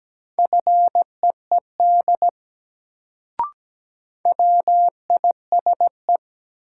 Morse code, heard as a keyed tone, about 17 wpm, 700 Hz. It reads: FEED WISE
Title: Sentences from Top 1600 Words - New Word Drill Alt - 17wpm